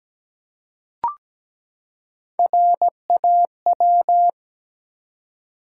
Morse code, heard as a keyed tone, about 17 wpm, 700 Hz. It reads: RAW
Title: Three Letter Words - 17wpm